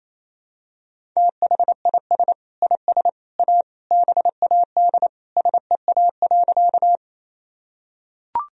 Morse code, heard as a keyed tone, about 28 wpm, 700 Hz. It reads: THIS IS A BAD SEA.
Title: Sentences from Top 1000 Words - Review Part 1 - 28wpm